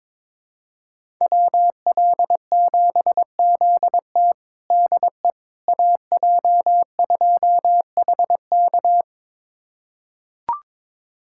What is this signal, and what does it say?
Morse code, heard as a keyed tone, about 22 wpm, 700 Hz. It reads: WL7ZT DE AJ2HK
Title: USA Callsign DE Callsign - 5 Characters - 22wpm